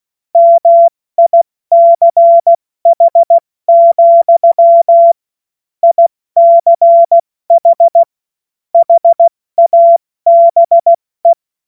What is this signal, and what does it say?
Morse code, heard as a keyed tone, about 16 wpm, 680 Hz. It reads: MICH, ICH HABE
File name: Morsekurs / Witze / Witz 01. Februar 2022 70 F80